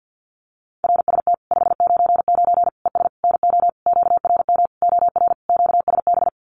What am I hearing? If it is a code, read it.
Morse code, heard as a keyed tone, about 50 wpm, 700 Hz. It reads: RST 599 ES NO QRM OR QSB
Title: Common QSO Phrases 50wpm